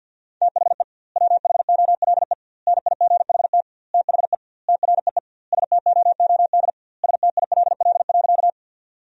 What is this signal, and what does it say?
Morse code, heard as a keyed tone, about 52 wpm, 700 Hz. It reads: THE WHOLE NIGHT THE TREE STOOD STILL.
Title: 2x Word-Spacing - Sentences from Top 500 Words - 50wpm